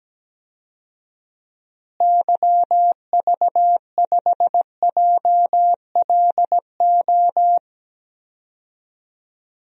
Morse code, heard as a keyed tone, about 17 wpm, 700 Hz. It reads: YV5JLO
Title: DXCC-100 Call Signs - 6 Characters - Repeated 3x - 17wpm